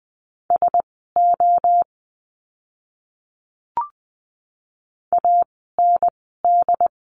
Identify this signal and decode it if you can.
Morse code, heard as a keyed tone, about 20 wpm, spaced as for 14 wpm, 700 Hz. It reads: SO AND
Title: 108 Lesson 18f - Letter m - Words 20wpm